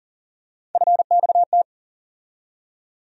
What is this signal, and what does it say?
Morse code, heard as a keyed tone, about 40 wpm, 700 Hz. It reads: FXT
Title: USA Call Sign Postfixes - 3 Letters 40wpm